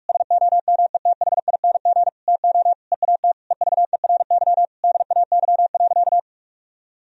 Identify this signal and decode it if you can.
Morse code, heard as a keyed tone, about 46 wpm, 700 Hz. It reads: SOMETHING TO EAT EVERY DAY.
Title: Sentences from Top 1000 Words - Review Part 4 - 45wpm